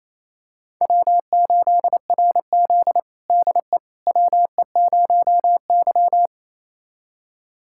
Morse code, heard as a keyed tone, about 28 wpm, 700 Hz. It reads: W8RZ DE WE0Y
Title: USA Callsign DE Callsign - 4 Characters - 28wpm